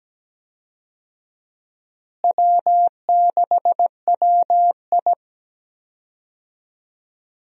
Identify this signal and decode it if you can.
Morse code, heard as a keyed tone, about 17 wpm, 700 Hz. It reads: W6WI
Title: USA Call Signs - 4 Characters - Repeated 2x - 17wpm